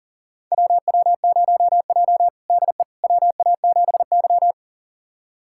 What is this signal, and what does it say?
Morse code, heard as a keyed tone, about 40 wpm, 700 Hz. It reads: WW0J DE WA7Y
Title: USA Callsign DE Callsign - 4 Characters - 40wpm